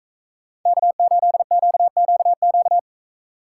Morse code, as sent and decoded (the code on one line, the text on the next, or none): -.- ---.. --.- --.- --.-
K8QQQ